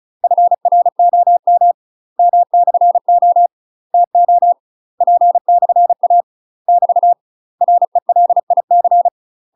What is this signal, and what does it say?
Morse code, heard as a keyed tone, about 34 wpm, 700 Hz. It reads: FROM M/O TO P/A <BT> RELIC
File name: UK MASTER QSOs TRAINING / 35WPM / UK MASTER QSOs 0019